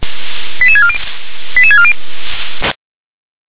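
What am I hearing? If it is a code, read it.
Five-tone selective call (ZVEI), heard as quick runs of five tones, tones of about 70 ms. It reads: 88640 88640
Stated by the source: ZVEI five-tone selective call 88910 received by SDR.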